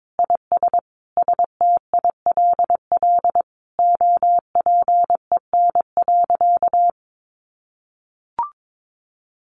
Morse code, heard as a keyed tone, about 22 wpm, 700 Hz. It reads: IS STILL OPEN.
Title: Sentences from Top 1500 Words 22wpm